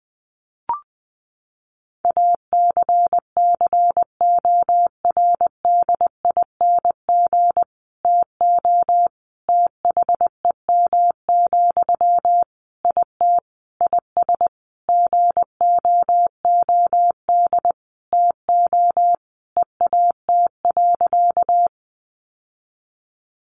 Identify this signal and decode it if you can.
Morse code, heard as a keyed tone, about 20 wpm, 700 Hz. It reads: ACCORDING TO THEM, IT IS GOOD TO EAT.
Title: Sentences from Top 800 Words 20wpm